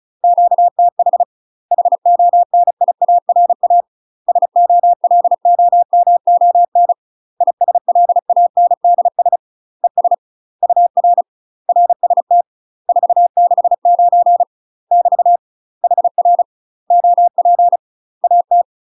Morse code, heard as a keyed tone, about 36 wpm, 700 Hz. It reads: QTH HONIARA SOLOMON ISLANDS ES UR RST 469 <BT> HR OP AT
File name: UK MASTER QSOs TRAINING / 35WPM / UK MASTER QSOs 0018